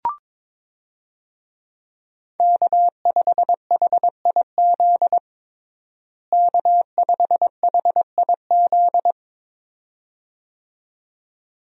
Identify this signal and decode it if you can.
Morse code, heard as a keyed tone, about 22 wpm, 700 Hz. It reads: K5HIZ K5HIZ
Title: USA Call Signs - 5 Characters - Repeated 2x - 22wpm